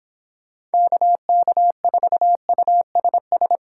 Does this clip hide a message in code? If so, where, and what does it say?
Yes: Morse code from 0.7 to 3.6 s, heard as a keyed tone, about 29 wpm, 700 Hz. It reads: KK4USS